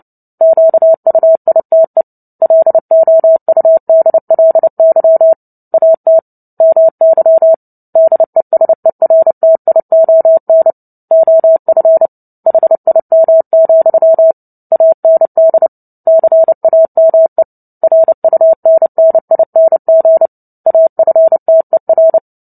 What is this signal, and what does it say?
Morse code, heard as a keyed tone, about 30 wpm, 650 Hz. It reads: QUITE LOUDLY AT MY DESERTION OF HIM, AND CAME RUNNING AFTER
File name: Book Two / 03-Chapter 3 - 29 WPM